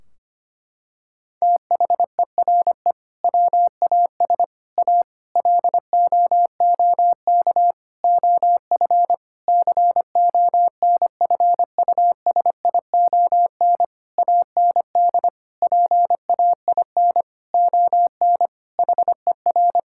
Morse code, heard as a keyed tone, about 25 wpm, 700 Hz. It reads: THERE WAS A LOOK OF CONFUSION AND PAIN ON HER
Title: Sentences from Top 1600 Words 25wpm